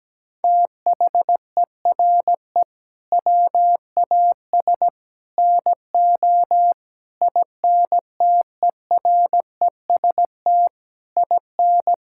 Morse code, heard as a keyed tone, about 17 wpm, 700 Hz. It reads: THERE WAS NO INTEREST IN
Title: Sentences from Top 1000 Words - Review Part 5 - 17wpm